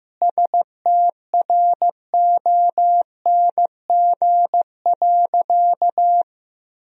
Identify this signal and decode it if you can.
Morse code, heard as a keyed tone, about 15 wpm, 700 Hz. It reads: STRONG.
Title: Sentences from Top 1000 Words - Review Part 3 - 15wpm